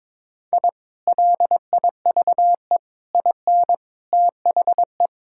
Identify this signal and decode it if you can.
Morse code, heard as a keyed tone, about 23 wpm, 700 Hz. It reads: I LIVE IN THE
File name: practice / 22wpm / Sentences with 2-Word Phrases - 22wpm